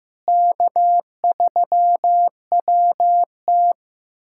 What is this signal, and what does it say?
Morse code, heard as a keyed tone, about 15 wpm, 700 Hz. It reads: K3WT
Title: USA Callsign DE Callsign - 4 Characters - 15wpm